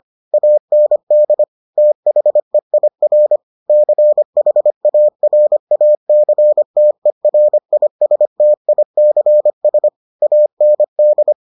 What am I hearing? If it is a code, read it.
Morse code, heard as a keyed tone, about 25 wpm, 600 Hz. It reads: AND THEIR CHARACTERISTICS AND